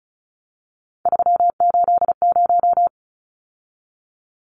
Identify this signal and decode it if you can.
Morse code, heard as a keyed tone, about 34 wpm, 700 Hz. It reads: 380
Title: Numbers - 3-Digit 35wpm